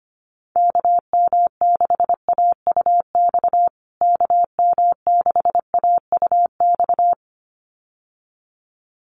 Morse code, heard as a keyed tone, about 25 wpm, 700 Hz. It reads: KM6AUX KM6AUX
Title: USA Call Signs Repeated 2x 25wpm